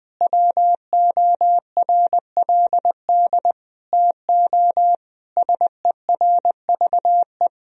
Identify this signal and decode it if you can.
Morse code, heard as a keyed tone, about 20 wpm, 700 Hz. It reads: WORLD TO SERVE
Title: Sentences from Top 1000 Words - Review Part 2 - 20wpm